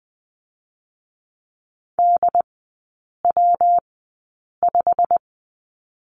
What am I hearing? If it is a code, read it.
Morse code, heard as a keyed tone, about 20 wpm, 700 Hz. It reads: D W 5
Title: 114 Lesson 19e - Letter w - Three Characters 20wpm